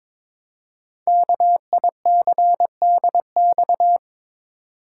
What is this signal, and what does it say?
Morse code, heard as a keyed tone, about 23 wpm, 700 Hz. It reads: KICDX
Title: Code Groups - 5 Letter:Numbers - 22wpm